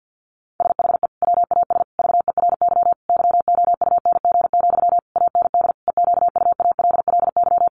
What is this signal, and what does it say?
Morse code, heard as a keyed tone, about 50 wpm, 700 Hz. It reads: SHE WAS VERY YOUNG, AND EQUALLY
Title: Sentences from Top 1400 Words 50wpm